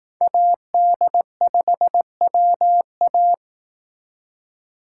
Morse code, heard as a keyed tone, about 18 wpm, 700 Hz. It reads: AD5WA
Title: Speed Racing - USA 2x2 Call Signs 15wpm